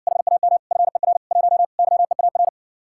Morse code, heard as a keyed tone, about 61 wpm, 700 Hz. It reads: HAM FER 1 YEAR